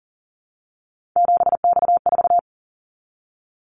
Morse code, heard as a keyed tone, about 37 wpm, 690 Hz. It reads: 7X4
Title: Callsign Prefixes for Top 100 DXCC - 40wpm